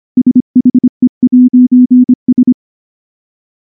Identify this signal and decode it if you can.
Morse code, heard as a keyed tone, about 23 wpm, 250 Hz. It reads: SHE'S
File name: morse / morse-training-25WPM